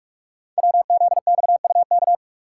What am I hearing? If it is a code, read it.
Morse code, heard as a keyed tone, about 44 wpm, 690 Hz. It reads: WZXUK
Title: Code Groups - 5 Letter:Numbers - 45wpm